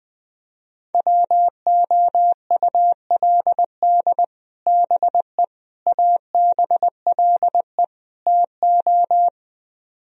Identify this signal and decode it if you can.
Morse code, heard as a keyed tone, about 20 wpm, 700 Hz. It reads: WOULD BE ABLE TO
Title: Speed-Racing - Sets of 4 Words 20wpm